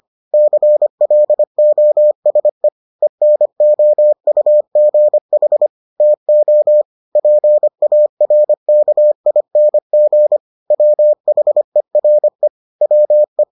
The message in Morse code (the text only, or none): CLOSE ENOUGH TO PARKING WHERE WE